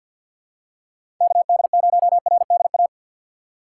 Morse code, heard as a keyed tone, about 52 wpm, 700 Hz. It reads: KD0RDA
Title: USA Call Signs - 6 Characters - 3x Character Spacing - 50wpm